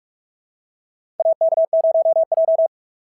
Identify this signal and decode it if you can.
Morse code, heard as a keyed tone, about 48 wpm, 690 Hz. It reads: AK0J
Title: SOTA QSO Practice 45wpm